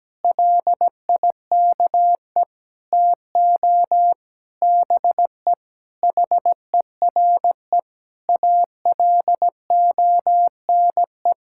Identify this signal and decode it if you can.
Morse code, heard as a keyed tone, about 17 wpm, 700 Hz. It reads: LIKE TO BE HERE ALONE
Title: Sentences from Top 1000 Words - Review Part 5 - 17wpm